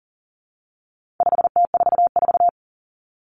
Morse code, heard as a keyed tone, about 40 wpm, 700 Hz. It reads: 5T44
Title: Common Cut Numbers - 4-Digit 40wpm